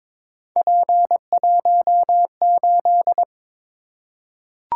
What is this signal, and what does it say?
Morse code, heard as a keyed tone, about 23 wpm, 700 Hz. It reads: P18
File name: practice / 22wpm / Three Letter-Number Combinations - 22wpm